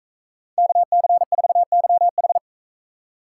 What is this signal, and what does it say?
Morse code, heard as a keyed tone, about 44 wpm, 700 Hz. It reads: KC4YH